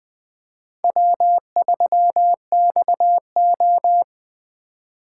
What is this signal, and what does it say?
Morse code, heard as a keyed tone, about 20 wpm, 700 Hz. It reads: W3XO